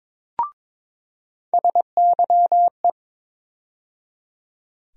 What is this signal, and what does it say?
Morse code, heard as a keyed tone, about 22 wpm, 700 Hz. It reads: SYE
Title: USA Call Sign Postfixes - 3 Letters 22wpm